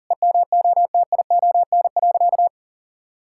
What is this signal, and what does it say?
Morse code, heard as a keyed tone, about 37 wpm, 700 Hz. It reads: EMOTION.
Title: Sentences from Top 1600 Words 40wpm